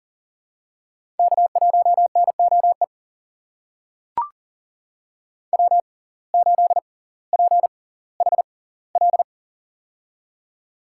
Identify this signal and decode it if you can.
Morse code, heard as a keyed tone, about 40 wpm, spaced as for 20 wpm, 700 Hz. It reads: K1NOE W8PHL
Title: USA Call Signs - 5 Characters - 6x Character Spacing - 40wpm